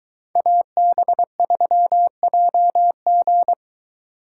Morse code, heard as a keyed tone, about 23 wpm, 700 Hz. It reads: AB3JG